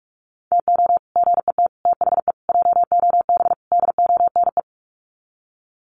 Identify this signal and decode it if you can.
Morse code, heard as a keyed tone, about 44 wpm, 690 Hz. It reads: TO GET THE JOB DONE
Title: Sets of 5 Words Encore2 45wpm